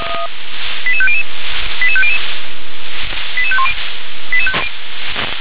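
Five-tone selective call (ZVEI), heard as quick runs of five tones, tones of about 70 ms. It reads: 88500 88500 88510
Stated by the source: ZVEI five-tone call received by SDR.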